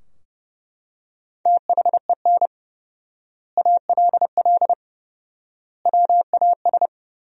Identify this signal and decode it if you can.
Morse code, heard as a keyed tone, about 31 wpm, 700 Hz. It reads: THEN ALL WAS
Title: 4x Word-Spacing - Sentences from Top 500 Words - 30wpm